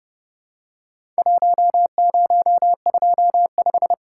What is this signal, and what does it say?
Morse code, heard as a keyed tone, about 30 wpm, 700 Hz. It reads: 1025
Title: Worked All Europe DX Contest - QTC Exchange - 30wpm